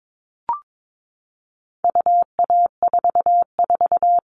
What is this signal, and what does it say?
Morse code, heard as a keyed tone, about 22 wpm, 700 Hz. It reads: UA44
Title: All Cut Numbers - 4-Digit 22wpm